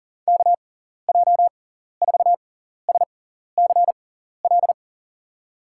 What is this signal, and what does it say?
Morse code, heard as a keyed tone, about 38 wpm, spaced as for 14 wpm, 700 Hz. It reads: KJ4SCL
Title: USA Call Signs - 6 Characters - 6x Character Spacing - 40wpm